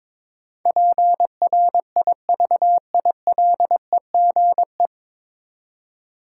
Morse code, heard as a keyed tone, about 22 wpm, 700 Hz. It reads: PRIVILEGE
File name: 9 Letter Words - First 3 Letters Given - 22wpm